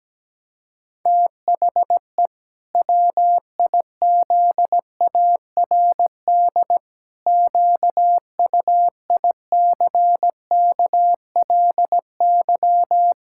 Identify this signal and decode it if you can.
Morse code, heard as a keyed tone, about 17 wpm, 700 Hz. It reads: THE WIZARD QUICKLY
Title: Pangram Challenge v2 - 17wpm